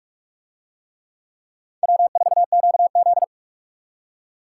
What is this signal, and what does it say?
Morse code, heard as a keyed tone, about 44 wpm, 700 Hz. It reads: W4QZ